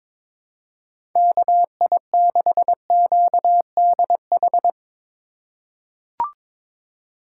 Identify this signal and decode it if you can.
Morse code, heard as a keyed tone, about 22 wpm, 700 Hz. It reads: KI6QDH
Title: K1USN SST QSO Practice - 22wpm